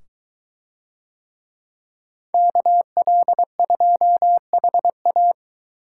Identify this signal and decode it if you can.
Morse code, heard as a keyed tone, about 21 wpm, 700 Hz. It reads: KL2HA